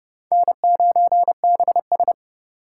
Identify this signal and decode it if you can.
Morse code, heard as a keyed tone, about 40 wpm, 700 Hz. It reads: N9BS